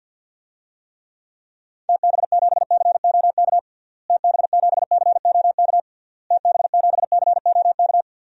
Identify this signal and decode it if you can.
Morse code, heard as a keyed tone, about 50 wpm, 700 Hz. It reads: TB7COK TB7COK TB7COK
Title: DXCC-100 Call Signs - 6 Characters - Repeated 3x - 50wpm